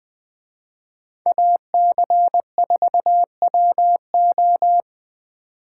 Morse code, heard as a keyed tone, about 20 wpm, 700 Hz. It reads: AC4WO